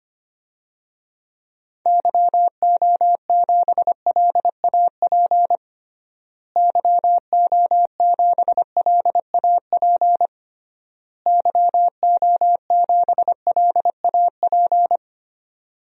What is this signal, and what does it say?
Morse code, heard as a keyed tone, about 25 wpm, 700 Hz. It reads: YO7LAP YO7LAP YO7LAP
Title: DXCC-100 Call Signs - 6 Characters - Repeated 3x - 25wpm